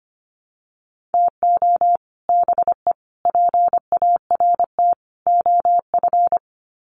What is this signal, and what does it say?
Morse code, heard as a keyed tone, about 25 wpm, 700 Hz. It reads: TO BE PART OF
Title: Sets of 4 Words 25wpm